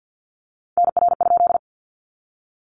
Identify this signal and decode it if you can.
Morse code, heard as a keyed tone, about 50 wpm, 700 Hz. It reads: NR?
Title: Common QSO Phrases 50wpm